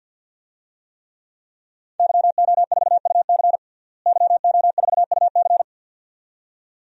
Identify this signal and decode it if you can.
Morse code, heard as a keyed tone, about 53 wpm, 700 Hz. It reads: YO4UC YO4UC
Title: DXCC-100 Call Signs - 5 Characters - Repeated 2x - 50wpm